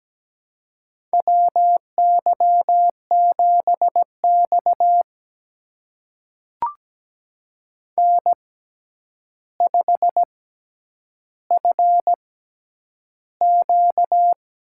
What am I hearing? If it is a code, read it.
Morse code, heard as a keyed tone, about 17 wpm, spaced as for 9 wpm, 700 Hz. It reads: WY7X N5FQ
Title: USA Call Signs - 4 Characters - 6x Character Spacing - 17wpm